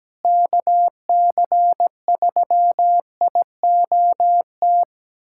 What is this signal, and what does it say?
Morse code, heard as a keyed tone, about 17 wpm, 700 Hz. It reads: KC3IOT
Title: USA Callsign DE Callsign - 6 Characters - 17wpm